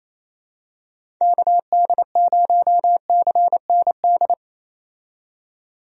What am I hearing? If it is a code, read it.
Morse code, heard as a keyed tone, about 28 wpm, 700 Hz. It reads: KD0CND
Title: USA Call Signs - 6 Characters - 28wpm